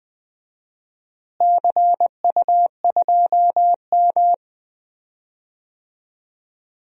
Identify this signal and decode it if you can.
Morse code, heard as a keyed tone, about 21 wpm, 700 Hz. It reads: CU2M
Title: DXCC-100 Call Signs - 4 Characters - Repeated 2x - 20wpm